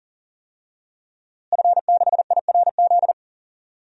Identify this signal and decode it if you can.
Morse code, heard as a keyed tone, about 40 wpm, 700 Hz. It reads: F6IRZ
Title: DXCC-100 Call Signs - 5 Characters - Repeated 2x - 40wpm